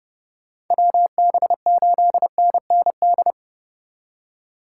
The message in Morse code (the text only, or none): WB8NND